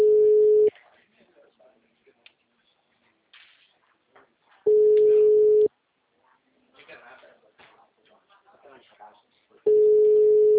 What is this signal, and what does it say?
Ringback tone (Europe, 425 Hz), with speech.
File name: out- -2040- - - .90663